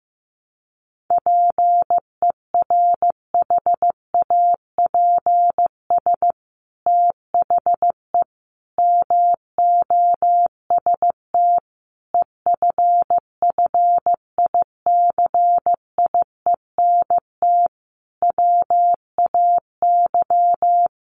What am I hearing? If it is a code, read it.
Morse code, heard as a keyed tone, about 15 wpm, 700 Hz. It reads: PERHAPS THE MOST EFFICIENT WAY
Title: Sentences from Top 1300 Words 15wpm